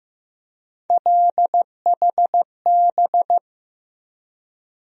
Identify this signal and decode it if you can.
Morse code, heard as a keyed tone, about 15 wpm, 700 Hz. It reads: LHB
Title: USA Call Sign Postfixes - 3 Letters 15wpm